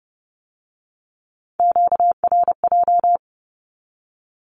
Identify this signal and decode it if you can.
Morse code, heard as a keyed tone, about 30 wpm, 700 Hz. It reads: QRJ
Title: Q-Codes with Spoken Meaning - 30wpm